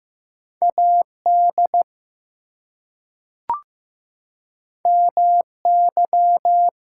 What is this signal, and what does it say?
Morse code, heard as a keyed tone, about 15 wpm, 700 Hz. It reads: AD MY
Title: Two Letter Words - 15wpm